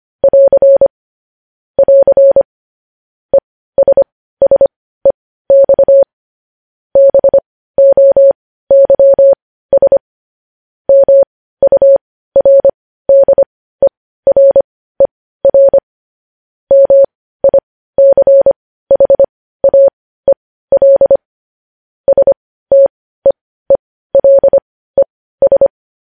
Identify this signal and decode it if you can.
Morse code, heard as a keyed tone, about 25 wpm, spaced as for 15 wpm, 550 Hz. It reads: <AR> <AR> ESSEX BOYS MURDERER MICHAEL STEELES
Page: news-headlines-in-morse-code-at-15-wpm